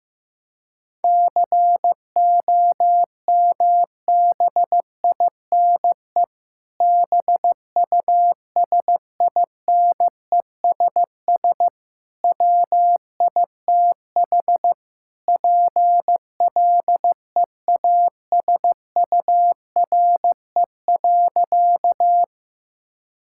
Morse code, heard as a keyed tone, about 15 wpm, 700 Hz. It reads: COMBINE BUSINESS WITH PLEASURE.
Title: Sentences from Top 1600 Words 15wpm